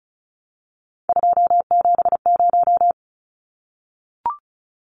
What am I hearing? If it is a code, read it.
Morse code, heard as a keyed tone, about 34 wpm, 700 Hz. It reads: 270
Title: Numbers - 3-Digit 35wpm